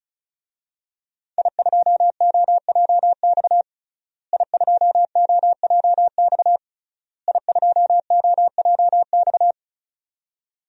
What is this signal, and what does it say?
Morse code, heard as a keyed tone, about 36 wpm, 690 Hz. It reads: I2OJX I2OJX I2OJX
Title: DXCC-100 Call Signs - 5 Characters - Repeated 3x - 35wpm